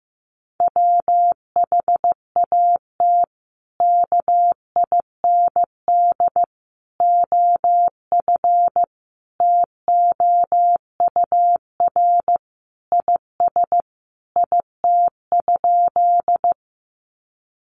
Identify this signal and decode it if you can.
Morse code, heard as a keyed tone, about 15 wpm, 700 Hz. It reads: WHAT KIND OF TOUR IS IT?
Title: Sentences from Top 1400 Words 15wpm